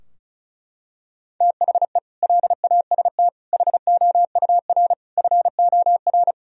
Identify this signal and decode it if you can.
Morse code, heard as a keyed tone, about 36 wpm, 700 Hz. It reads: THE LAST HOUR FOR
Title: Sentences from Top 1000 Words - Review Part 4 - 35wpm